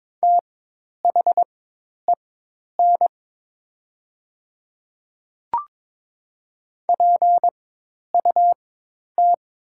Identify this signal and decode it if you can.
Morse code, heard as a keyed tone, about 22 wpm, spaced as for 10 wpm, 700 Hz. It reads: THEN PUT
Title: 4x Character Spacing - Top 300 Words - 22wpm